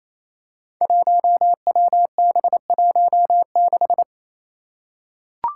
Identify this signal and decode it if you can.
Morse code, heard as a keyed tone, about 28 wpm, 700 Hz. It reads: 1WB16
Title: Five Challenging Letter-Number Combinations 28wpm